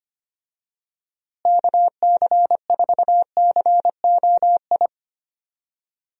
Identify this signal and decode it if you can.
Morse code, heard as a keyed tone, about 25 wpm, 700 Hz. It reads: KC4COI
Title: USA Call Signs - 6 Characters - Repeated 2x - 25wpm